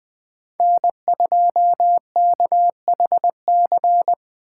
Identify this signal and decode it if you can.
Morse code, heard as a keyed tone, about 20 wpm, 700 Hz. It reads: N2KHC
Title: USA Call Signs - 2x Character Spacing - 20wpm